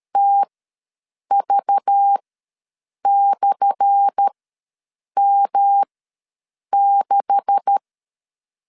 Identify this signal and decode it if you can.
Morse code, heard as a keyed tone, about 13 wpm, spaced as for 7 wpm, 800 Hz. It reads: TV/M6
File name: LEZIONI / 10 SIGNS / 10A SLASH ws 12wpm 3443